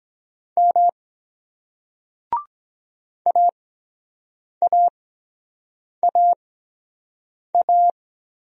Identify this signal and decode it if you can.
Morse code, heard as a keyed tone, about 23 wpm, spaced as for 7 wpm, 700 Hz. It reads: M AAAA